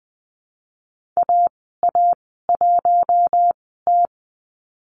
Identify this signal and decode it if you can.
Morse code, heard as a keyed tone, about 20 wpm, spaced as for 15 wpm, 700 Hz. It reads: AA1T
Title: 042 Lesson 8g - Number 1 - Call Signs 20wpm